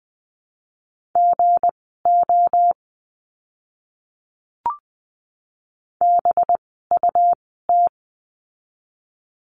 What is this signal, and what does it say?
Morse code, heard as a keyed tone, about 20 wpm, spaced as for 15 wpm, 700 Hz. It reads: GO BUT
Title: 206 Lesson 33f - Letter j - Words 20wpm